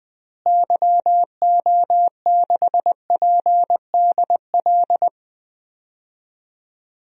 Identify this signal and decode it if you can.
Morse code, heard as a keyed tone, about 20 wpm, 700 Hz. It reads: YO6PDL
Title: DXCC-100 Call Signs - 6 Characters - Repeated 2x - 20wpm